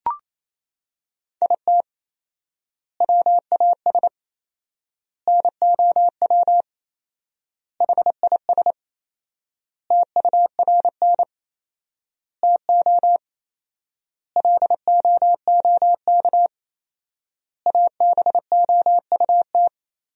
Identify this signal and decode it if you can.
Morse code, heard as a keyed tone, about 27 wpm, 700 Hz. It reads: IT WAS NOW HIS TURN TO LOOK ABOUT
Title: 4x Word-Spacing - Sentences from Top 200 Words - 28wpm